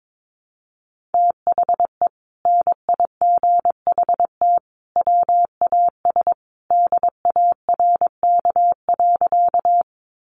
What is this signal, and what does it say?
Morse code, heard as a keyed tone, about 22 wpm, 700 Hz. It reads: THE NIGHT WAS DARK.
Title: Sentences from Top 500 Words 22wpm